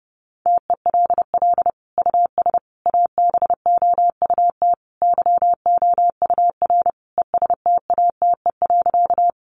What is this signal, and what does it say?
Morse code, heard as a keyed tone, about 30 wpm, 700 Hz. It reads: TELL US ABOUT YOUR ESTATE.
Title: Sentences from Top 1200 Words 30wpm